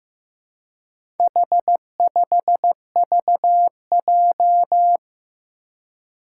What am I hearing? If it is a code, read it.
Morse code, heard as a keyed tone, about 15 wpm, 700 Hz. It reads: H5VJ
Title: Four Challenging Letter-Number Combinations 15wpm